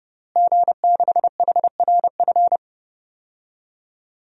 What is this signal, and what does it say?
Morse code, heard as a keyed tone, about 32 wpm, 700 Hz. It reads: G6HRF
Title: DXCC-100 Call Signs - 5 Characters - Repeated 2x - 30wpm